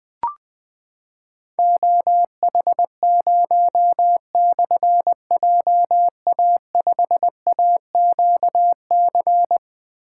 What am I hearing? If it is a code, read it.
Morse code, heard as a keyed tone, about 20 wpm, 700 Hz. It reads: OH0/JA5AQC
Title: Difficult Call Signs 20wpm